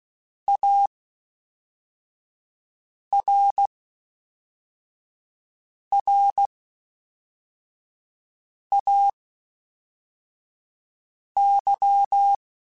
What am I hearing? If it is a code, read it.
Morse code, heard as a keyed tone, about 16 wpm, spaced as for 4 wpm, 800 Hz. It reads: ARRAY
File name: (spaced) / FromSMs spaced mp3s / Classes@5